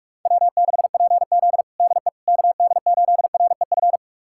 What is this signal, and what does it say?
Morse code, heard as a keyed tone, about 46 wpm, 700 Hz. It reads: W6PZ DE KD8REF
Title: USA Callsign DE Callsign 45wpm